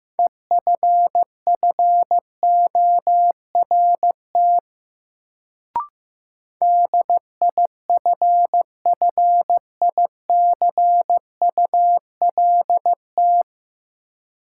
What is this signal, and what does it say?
Morse code, heard as a keyed tone, about 15 wpm, 700 Hz. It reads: EFFORT DIFFICULT
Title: Sets of 4 Words - New Word Drill - 15wpm